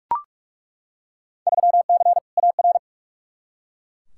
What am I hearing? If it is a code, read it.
Morse code, heard as a keyed tone, about 41 wpm, 700 Hz. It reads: 3C AR
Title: Field Day Exchanges Encore - 45wpm